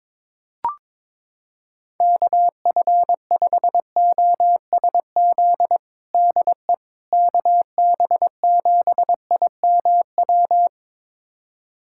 Morse code, heard as a keyed tone, about 21 wpm, 700 Hz. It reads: KF5OSZ DE KB7IMW
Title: USA Callsign DE Callsign 22wpm